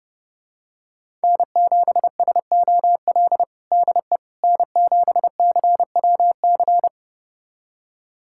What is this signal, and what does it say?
Morse code, heard as a keyed tone, about 30 wpm, 700 Hz. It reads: N7SOL DE N7CWC
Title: USA Callsign DE Callsign - 5 Characters - 30wpm